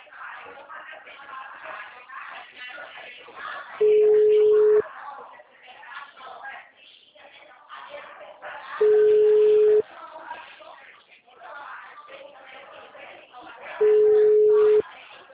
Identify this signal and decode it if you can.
Ringback tone (Europe, 425 Hz), with speech.
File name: out- -2010- - - .35011